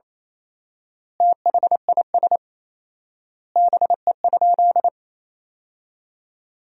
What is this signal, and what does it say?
Morse code, heard as a keyed tone, about 28 wpm, 700 Hz. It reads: THIS BE?
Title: 4x Word-Spacing - Sentences from Top 100 Words - 28wpm